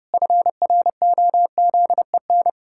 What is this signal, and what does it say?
Morse code, heard as a keyed tone, about 30 wpm, 700 Hz. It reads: FROZEN
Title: Pangram Challenge v2 - 30wpm